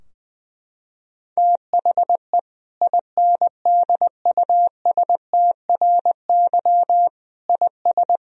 Morse code, heard as a keyed tone, about 20 wpm, 700 Hz. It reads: THE INDUSTRY IS
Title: Sentences from Top 1000 Words - Review Part 4 - 20wpm